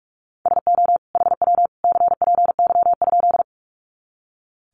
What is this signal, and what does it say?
Morse code, heard as a keyed tone, about 44 wpm, 700 Hz. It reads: SO HW CPY?
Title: Common QSO Phrases 45wpm